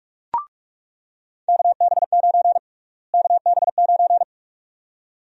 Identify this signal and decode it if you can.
Morse code, heard as a keyed tone, about 46 wpm, 700 Hz. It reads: KB9 KB9
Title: USA Call Sign Prefixes - Repeated 2x - 45wpm